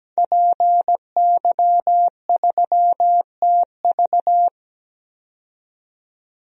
Morse code, heard as a keyed tone, about 17 wpm, 700 Hz. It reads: PY3TV
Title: DXCC-100 Call Signs - 5 Characters - 17wpm